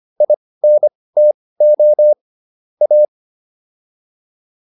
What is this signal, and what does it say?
Morse code, heard as a keyed tone, about 25 wpm, spaced as for 18 wpm, 600 Hz. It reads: INTO A
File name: WD104 18